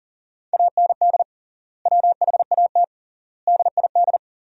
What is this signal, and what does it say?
Morse code, heard as a keyed tone, about 37 wpm, 700 Hz. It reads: AND WHAT DID
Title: 3x Word-Spacing - Sentences from Top 100 Words - 40wpm